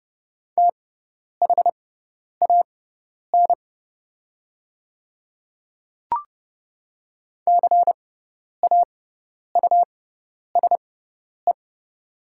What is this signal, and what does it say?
Morse code, heard as a keyed tone, about 30 wpm, spaced as for 10 wpm, 690 Hz. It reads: THAN CAUSE
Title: 6x Character Spacing - Top 200 Words - 30wpm